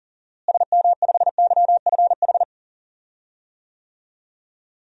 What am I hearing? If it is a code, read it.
Morse code, heard as a keyed tone, about 37 wpm, 700 Hz. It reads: SM5YFH
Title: DXCC-100 Call Signs - 6 Characters - Repeated 2x - 40wpm